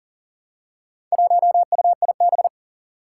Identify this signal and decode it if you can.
Morse code, heard as a keyed tone, about 40 wpm, 700 Hz. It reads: 1UIB
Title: Four Challenging Letter-Number Combinations 40wpm